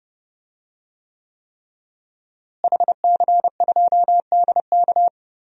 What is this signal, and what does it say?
Morse code, heard as a keyed tone, about 30 wpm, 700 Hz. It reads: HC2DK
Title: DXCC-100 Call Signs - 5 Characters - Repeated 3x - 30wpm